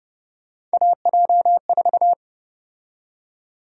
Morse code, heard as a keyed tone, about 30 wpm, 700 Hz. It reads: AJ4
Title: USA Call Sign Prefixes - Repeated 2x - 30wpm